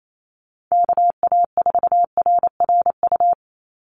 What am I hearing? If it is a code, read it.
Morse code, heard as a keyed tone, about 28 wpm, 700 Hz. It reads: KA4RRU
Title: Call Signs Repeated 3x - 28wpm